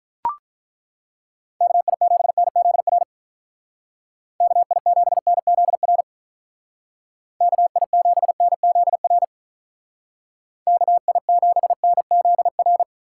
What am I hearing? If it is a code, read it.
Morse code, heard as a keyed tone, about 43 wpm, 700 Hz. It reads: KI7NZR KI7NZR KI7NZR KI7NZR
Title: Speed Racing - USA 2x3 Call Signs 35wpm